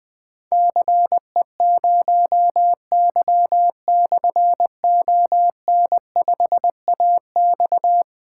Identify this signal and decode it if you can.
Morse code, heard as a keyed tone, about 20 wpm, 700 Hz. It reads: CE0Y/ON5AX
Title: Difficult Call Signs 20wpm